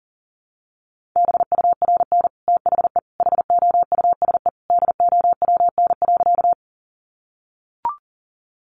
Morse code, heard as a keyed tone, about 40 wpm, 700 Hz. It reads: BURN THE HOUSE DOWN.
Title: Sentences from Top 1300 Words 40wpm